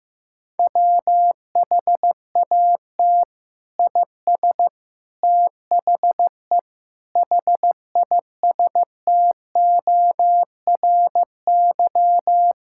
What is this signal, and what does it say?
Morse code, heard as a keyed tone, about 15 wpm, 700 Hz. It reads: WHAT IS THE HISTORY
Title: Sentences from Top 1000 Words - Review Part 5 - 15wpm